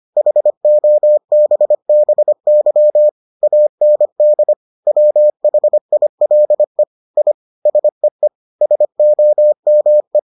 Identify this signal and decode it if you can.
Morse code, heard as a keyed tone, about 25 wpm, 600 Hz. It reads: HOBBY AND WHILE I SEE SOME